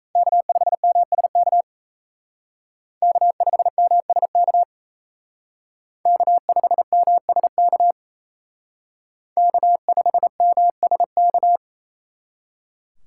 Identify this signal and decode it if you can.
Morse code, heard as a keyed tone, about 36 wpm, 700 Hz. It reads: K5MSK K5MSK K5MSK K5MSK